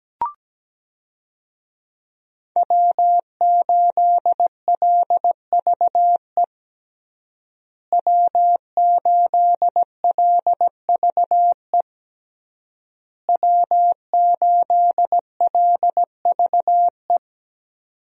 Morse code, heard as a keyed tone, about 17 wpm, 700 Hz. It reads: W8LVE W8LVE W8LVE
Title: USA Call Signs - 5 Characters - Repeated 3x - 17wpm